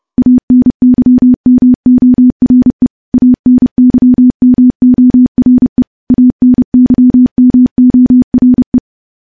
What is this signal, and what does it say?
Morse code, heard as a keyed tone, about 30 wpm, 250 Hz. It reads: ANYMORE ANYMORE ANYMORE
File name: morse / morse-training-30WPM